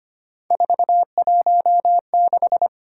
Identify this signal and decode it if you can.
Morse code, heard as a keyed tone, about 25 wpm, 700 Hz. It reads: 416
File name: Three Letter-Number Combinations - 25wpm